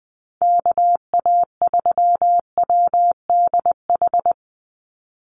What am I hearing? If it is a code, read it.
Morse code, heard as a keyed tone, about 20 wpm, 700 Hz. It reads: KA3WDH
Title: USA Call Signs 20wpm